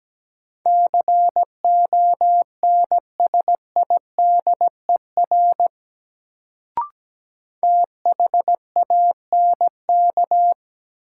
Morse code, heard as a keyed tone, about 17 wpm, 700 Hz. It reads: CONSIDER THANK
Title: Sentences from Top 600 Words - New Word Drill - 17wpm